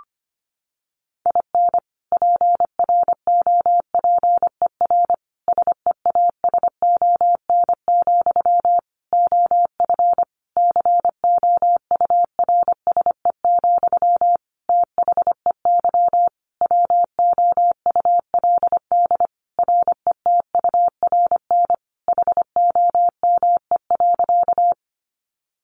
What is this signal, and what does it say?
Morse code, heard as a keyed tone, about 25 wpm, 700 Hz. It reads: IN PROPER SEASON, OF COURSE, THEY WOULD RETURN HOME.
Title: Sentences from Top 1400 Words 25wpm